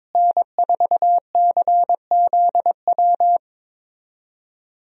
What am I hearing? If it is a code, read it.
Morse code, heard as a keyed tone, about 22 wpm, 700 Hz. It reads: N4CZW
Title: USA Call Signs - 5 Characters - Repeated 2x - 22wpm